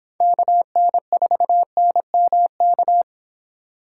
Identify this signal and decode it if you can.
Morse code, heard as a keyed tone, about 25 wpm, 700 Hz. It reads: KN4NMK